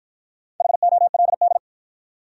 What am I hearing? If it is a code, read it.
Morse code, heard as a keyed tone, about 53 wpm, 700 Hz. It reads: HOLD